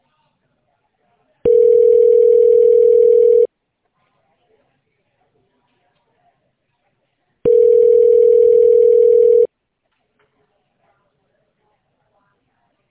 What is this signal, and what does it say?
Ringback tone (North America), 440 + 480 Hz.